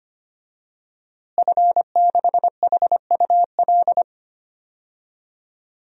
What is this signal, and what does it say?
Morse code, heard as a keyed tone, about 25 wpm, 700 Hz. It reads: F6HUL
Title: DXCC-100 Call Signs - 5 Characters - Repeated 2x - 25wpm